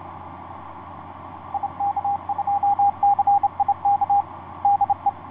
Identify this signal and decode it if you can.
Morse code, heard as a keyed tone, about 29 wpm, 810 Hz, with radio noise. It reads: IK2CIK DE
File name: ys1zc ik2cik etc 30m cw 07feb2010 2249